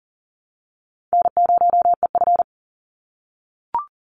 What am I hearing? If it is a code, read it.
Morse code, heard as a keyed tone, about 40 wpm, 700 Hz. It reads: N0EF
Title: Call Signs - 40wpm